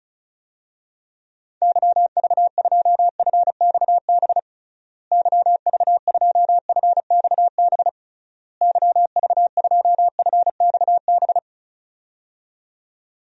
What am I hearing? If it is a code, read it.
Morse code, heard as a keyed tone, about 34 wpm, 700 Hz. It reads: YV2FXB YV2FXB YV2FXB
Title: DXCC-100 Call Signs - 6 Characters - Repeated 3x - 35wpm